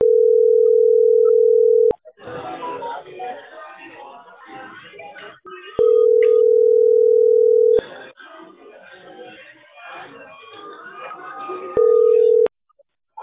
Ringback tone (North America), 440 + 480 Hz.